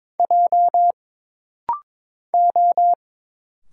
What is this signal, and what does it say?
Morse code, heard as a keyed tone, about 22 wpm, 700 Hz. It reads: J O
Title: Single Letter-Number Alt - Rapid-Fire 22wpm